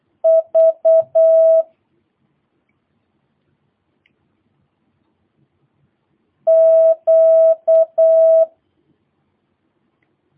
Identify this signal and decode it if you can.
Morse code, heard as a keyed tone, about 8 wpm, 650 Hz. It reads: V Q
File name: jh8cbh / 2022cw